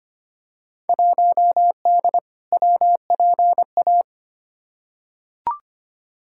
Morse code, heard as a keyed tone, about 25 wpm, 700 Hz. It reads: 1D WPA
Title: Field Day Exchanges - 25wpm